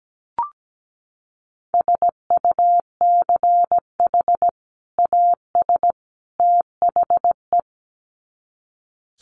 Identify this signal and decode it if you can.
Morse code, heard as a keyed tone, about 17 wpm, 700 Hz. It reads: SUCH AS THE
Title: Sets of 3 Words 17wpm